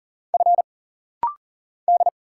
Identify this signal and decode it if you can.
Morse code, heard as a keyed tone, about 40 wpm, 700 Hz. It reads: F D
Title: Single Letters - Rapid-Fire 40wpm